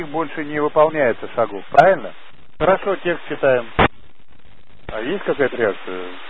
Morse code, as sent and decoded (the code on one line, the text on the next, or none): none